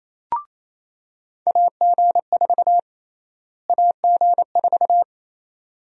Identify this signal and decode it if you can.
Morse code, heard as a keyed tone, about 28 wpm, 700 Hz. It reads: AG4 AG4
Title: USA Call Sign Prefixes - Repeated 2x - 28wpm